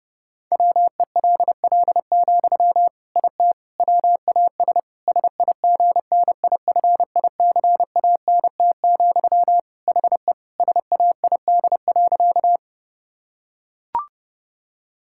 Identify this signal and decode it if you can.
Morse code, heard as a keyed tone, about 30 wpm, 700 Hz. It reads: WELL, IT WAS SIGNIFICANT, HE SAID.
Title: Sentences from Top 1000 Words - Review Part 6 - 30wpm